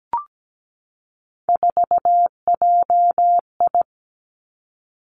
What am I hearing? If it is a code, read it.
Morse code, heard as a keyed tone, about 17 wpm, 700 Hz. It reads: 4JI
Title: Three Challenging Letter-Number Combinations 17wpm